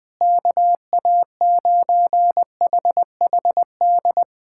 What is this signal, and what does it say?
Morse code, heard as a keyed tone, about 20 wpm, 700 Hz. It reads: KA9HHD
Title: USA Callsign DE Callsign 20wpm